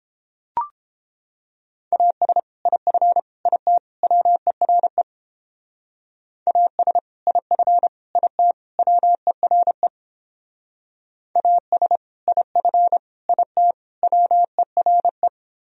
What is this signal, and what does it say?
Morse code, heard as a keyed tone, about 29 wpm, 700 Hz. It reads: AS IF IT WERE AS IF IT WERE AS IF IT WERE